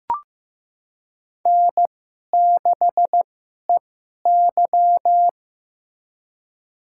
Morse code, heard as a keyed tone, about 15 wpm, spaced as for 11 wpm, 700 Hz. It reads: N6EY
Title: USA Call Signs - 4 Characters - 2x Character Spacing - 15wpm